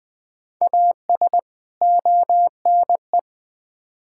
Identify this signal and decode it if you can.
Morse code, heard as a keyed tone, about 20 wpm, 700 Hz. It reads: AS ONE
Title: Build A Phrase - 5-Word Phrases - 20wpm